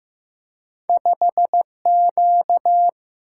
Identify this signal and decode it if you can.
Morse code, heard as a keyed tone, about 15 wpm, 700 Hz. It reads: 5Q
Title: Single-Letter Single-Number 15wpm